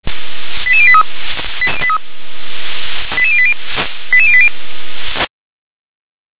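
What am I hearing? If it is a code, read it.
Five-tone selective call (ZVEI), heard as quick runs of five tones, tones of about 70 ms. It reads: 88083 88080 88080
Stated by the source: ZVEI five-tone selective call 88747 received on an SDR.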